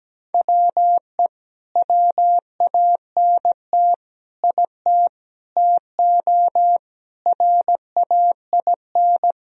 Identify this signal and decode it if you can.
Morse code, heard as a keyed tone, about 17 wpm, 700 Hz. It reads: WE WANT IT TO RAIN
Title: Sentences from Top 1000 Words - Review Part 1 - 17wpm